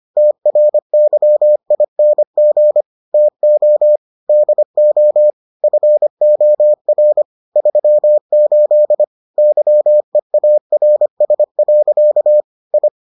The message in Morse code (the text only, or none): TRYING TO DO FOR 38 YEARS. I